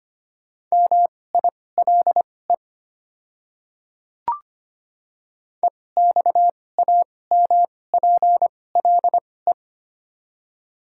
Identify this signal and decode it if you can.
Morse code, heard as a keyed tone, about 25 wpm, spaced as for 18 wpm, 700 Hz. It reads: MILE EXAMPLE
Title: 2x Character Spacing - Top 300 Words - 25wpm